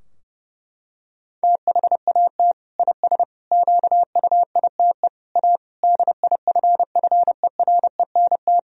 Morse code, heard as a keyed tone, about 32 wpm, 700 Hz. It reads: THAT IS QUITE A DIFFERENT
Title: Sentences from Top 1000 Words - Review Part 3 - 30wpm